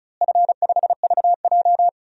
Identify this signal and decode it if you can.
Morse code, heard as a keyed tone, about 36 wpm, 690 Hz. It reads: F5VJ
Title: DXCC-100 Call Signs - 4 Characters - Repeated 2x - 35wpm